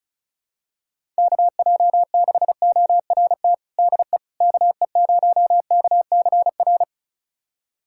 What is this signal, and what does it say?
Morse code, heard as a keyed tone, about 36 wpm, 700 Hz. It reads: KJ6ORT DE KE0KCR
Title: USA Callsign DE Callsign - 6 Characters - 35wpm